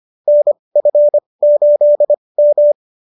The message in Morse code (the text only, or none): NF8M